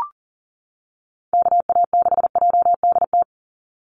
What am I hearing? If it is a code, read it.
Morse code, heard as a keyed tone, about 40 wpm, 700 Hz. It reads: KA6JDT
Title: USA Call Signs 40wpm